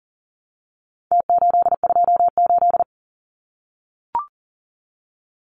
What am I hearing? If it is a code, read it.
Morse code, heard as a keyed tone, about 40 wpm, 700 Hz. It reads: T828
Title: Numbers - Contest Style - 4-Digit 40wpm